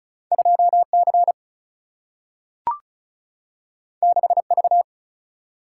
Morse code, heard as a keyed tone, about 36 wpm, 700 Hz. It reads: 2 C 6 V
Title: Single-Letter Single-Number 35wpm